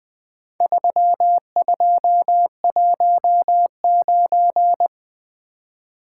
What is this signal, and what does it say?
Morse code, heard as a keyed tone, about 20 wpm, 700 Hz. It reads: 3219
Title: Four Letter-Number Combinations - 20wpm